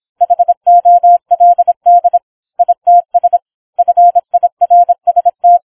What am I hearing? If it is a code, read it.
Morse code, heard as a keyed tone, about 26 wpm, 700 Hz. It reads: HOLD ITS FIRST